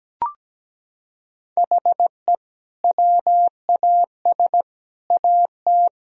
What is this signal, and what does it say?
Morse code, heard as a keyed tone, about 17 wpm, 700 Hz. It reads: HE WAS AT
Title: Sentences from Top 200 Words and Abbreviations 17wpm